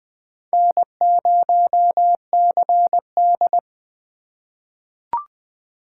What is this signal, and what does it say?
Morse code, heard as a keyed tone, about 20 wpm, 700 Hz. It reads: N0CD
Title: USA Call Signs - 4 Characters - 4x Character Spacing - 20wpm